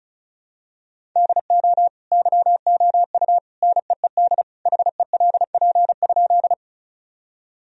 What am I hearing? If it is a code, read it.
Morse code, heard as a keyed tone, about 34 wpm, 700 Hz. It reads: DO YOU NEED HELP?
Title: Sentences from Top 1000 Words - Review Part 1 - 35wpm